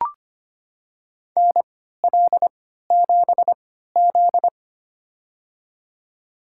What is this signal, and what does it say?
Morse code, heard as a keyed tone, about 25 wpm, spaced as for 14 wpm, 700 Hz. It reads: NL7Z
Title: USA Call Signs - 4 Characters - 3x Character Spacing - 25wpm